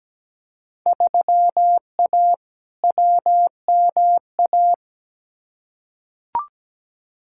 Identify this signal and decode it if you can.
Morse code, heard as a keyed tone, about 17 wpm, 700 Hz. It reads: 3A WMA
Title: Field Day Exchanges Encore - 17wpm